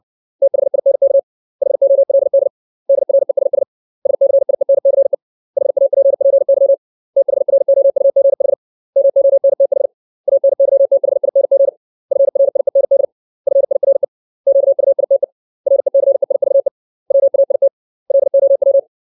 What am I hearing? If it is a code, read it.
Morse code, heard as a keyed tone, about 60 wpm, 550 Hz. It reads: THINK HOLD DRESS SCIENCE HAPPY THROUGH MONTH ANYTHING FRIEND FIRE QUITE RECEIVE WAIT LOW
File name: 632 words / 632 Words 60WPM